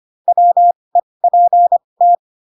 Morse code, heard as a keyed tone, about 25 wpm, spaced as for 20 wpm, 700 Hz. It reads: WEPT